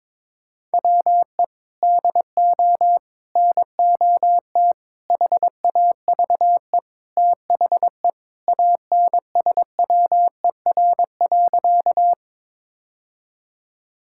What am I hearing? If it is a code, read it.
Morse code, heard as a keyed tone, about 22 wpm, 700 Hz. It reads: WE DO NOT HAVE THE ANSWER.
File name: practice / 22wpm / Sentences with 4-Word Phrases - 22wpm